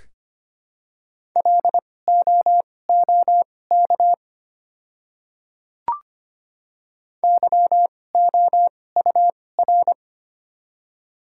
Morse code, heard as a keyed tone, about 25 wpm, spaced as for 18 wpm, 700 Hz. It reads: LOOK YOUR
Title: 2x Character Spacing - Top 100 Words - 25wpm